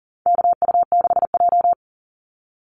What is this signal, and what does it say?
Morse code, heard as a keyed tone, about 40 wpm, 700 Hz. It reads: KU6J
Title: NA SOTA Chaser Callsigns 40wpm